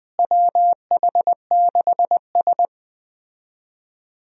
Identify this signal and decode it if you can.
Morse code, heard as a keyed tone, about 20 wpm, 700 Hz. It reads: WH6S
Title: USA Call Signs - 4 Characters - Repeated 2x - 20wpm